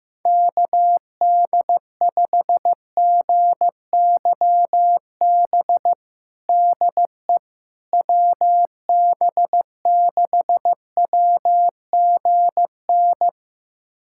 Morse code, heard as a keyed tone, about 15 wpm, 700 Hz. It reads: KD5GYB DE WB6WGN
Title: USA Callsign DE Callsign - 6 Characters - 15wpm